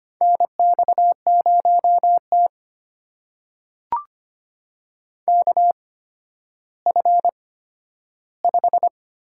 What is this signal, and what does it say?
Morse code, heard as a keyed tone, about 25 wpm, spaced as for 12 wpm, 700 Hz. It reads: NX0T KF5
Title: USA Call Signs - 8x Character Spacing - 25wpm